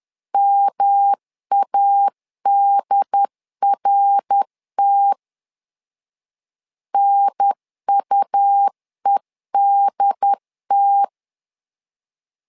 Morse code, heard as a keyed tone, about 10 wpm, 800 Hz. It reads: MADRT NUEDT
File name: LEZIONI / 3 MWRO / 3 MWR 10wpm 1952